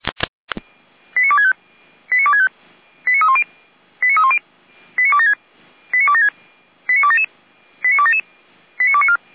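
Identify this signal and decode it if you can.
Five-tone selective call (ZVEI), heard as quick runs of five tones, tones of about 70 ms. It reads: 89265 89265 89310 89276 89276 89277 89277 89294
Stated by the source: ZVEI five-tone call 89173 received by SDR, logged by Feuerwehr Orlamünde.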